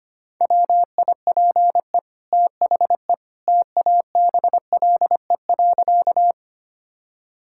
Morse code, heard as a keyed tone, about 25 wpm, 700 Hz. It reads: WIPE THE TABLE.
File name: Easy Sentences with Hints - 25wpm